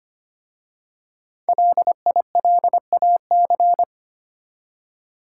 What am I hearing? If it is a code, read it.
Morse code, heard as a keyed tone, about 25 wpm, 700 Hz. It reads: LILAC
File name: practice / 25wpm / Easy Words with Hints - 25wpm